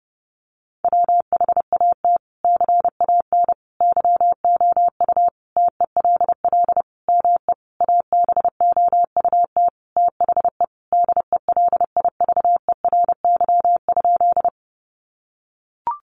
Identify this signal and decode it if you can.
Morse code, heard as a keyed tone, about 30 wpm, 700 Hz. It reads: WHAT CAN YOU TELL ME ABOUT THE DELIVERY?
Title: Sentences from Top 1500 Words 30wpm